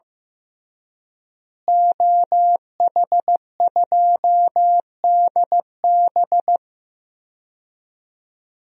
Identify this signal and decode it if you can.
Morse code, heard as a keyed tone, about 15 wpm, 700 Hz. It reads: OH2DB
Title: DXCC-100 Call Signs - 5 Characters - Repeated 2x - 15wpm